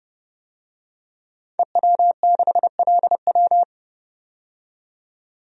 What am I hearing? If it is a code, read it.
Morse code, heard as a keyed tone, about 30 wpm, 700 Hz. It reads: EW6LW
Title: DXCC-100 Call Signs - 5 Characters - 30wpm